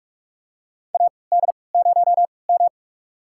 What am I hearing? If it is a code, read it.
Morse code, heard as a keyed tone, about 47 wpm, spaced as for 26 wpm, 700 Hz. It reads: AD0M
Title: USA Call Signs - 4 Characters - 3x Character Spacing - 45wpm